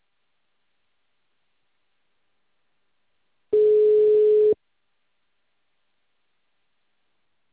Ringback tone (Europe, 425 Hz).